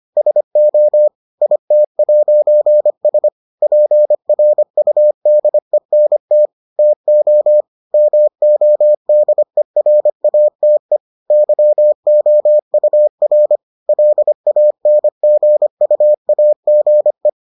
Morse code, heard as a keyed tone, about 25 wpm, 600 Hz. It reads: SO IT'S PRUDENT TO MODERATE YOUR LANGUAGE